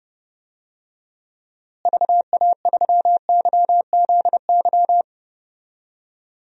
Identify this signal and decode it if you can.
Morse code, heard as a keyed tone, about 30 wpm, 700 Hz. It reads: VA3YZY
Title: DXCC-100 Call Signs - 6 Characters - Repeated 2x - 30wpm